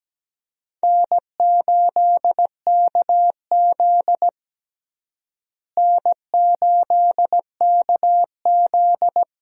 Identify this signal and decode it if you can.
Morse code, heard as a keyed tone, about 17 wpm, 700 Hz. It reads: N8KZ N8KZ
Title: USA Call Signs - 4 Characters - Repeated 2x - 17wpm